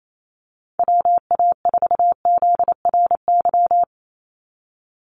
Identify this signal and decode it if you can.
Morse code, heard as a keyed tone, about 28 wpm, 700 Hz. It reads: WA4ZRY
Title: USA Call Signs 28wpm